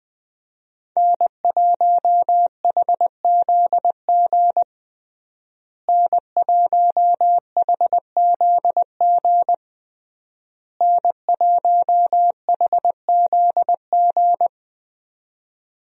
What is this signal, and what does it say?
Morse code, heard as a keyed tone, about 20 wpm, 700 Hz. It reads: N1HZG N1HZG N1HZG
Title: USA Call Signs - 5 Characters - Repeated 3x - 20wpm